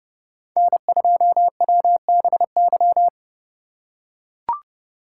Morse code, heard as a keyed tone, about 30 wpm, 700 Hz. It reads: N2WBY
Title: USA Call Signs - 5 Characters - 3x Character Spacing - 30wpm